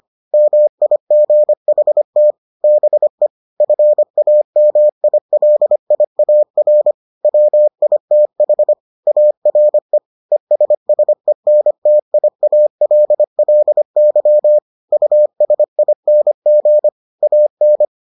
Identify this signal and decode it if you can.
Morse code, heard as a keyed tone, about 25 wpm, 600 Hz. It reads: MIGHT BE FAMILIAR WITH ARE ESSENTIALLY USING AN